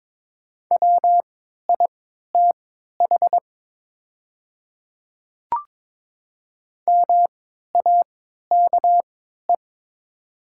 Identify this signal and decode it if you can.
Morse code, heard as a keyed tone, about 22 wpm, spaced as for 12 wpm, 700 Hz. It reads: WITH MAKE
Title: 3x Character Spacing - Top 100 Words - 22wpm